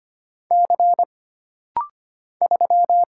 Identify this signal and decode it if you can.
Morse code, heard as a keyed tone, about 25 wpm, 700 Hz. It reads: C 3
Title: Single Letter-Number Alt - Rapid-Fire 25wpm